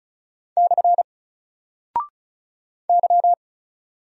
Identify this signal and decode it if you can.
Morse code, heard as a keyed tone, about 36 wpm, 700 Hz. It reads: / Y
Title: Single Character (Letter:Number:Punctuation:Prosign) 35wpm